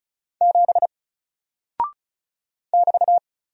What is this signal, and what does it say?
Morse code, heard as a keyed tone, about 36 wpm, 700 Hz. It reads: 7 <BT>
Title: Single Character (Letter:Number:Punctuation:Prosign) Alt 35wpm